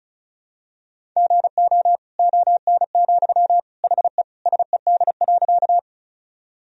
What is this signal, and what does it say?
Morse code, heard as a keyed tone, about 36 wpm, 700 Hz. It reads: GO ON, HE SED.
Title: Sentences from Top 100 Words and Abbreviations 35wpm